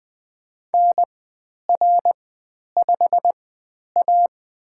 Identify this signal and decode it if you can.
Morse code, heard as a keyed tone, about 20 wpm, spaced as for 10 wpm, 700 Hz. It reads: NR5A
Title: 109 Lesson 18g - Letter m - Call Signs 20wpm